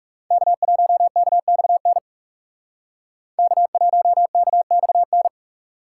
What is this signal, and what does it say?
Morse code, heard as a keyed tone, about 39 wpm, 700 Hz. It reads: K1KXN K1KXN